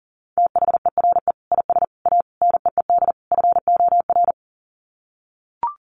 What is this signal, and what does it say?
Morse code, heard as a keyed tone, about 37 wpm, 700 Hz. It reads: THERE IS A NEED FOR
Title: Sets of 5 Words Encore2 40wpm